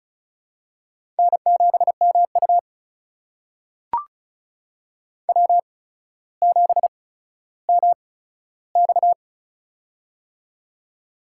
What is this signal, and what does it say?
Morse code, heard as a keyed tone, about 36 wpm, spaced as for 15 wpm, 700 Hz. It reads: N7MU W7MX
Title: USA Call Signs - 4 Characters - 8x Character Spacing - 35wpm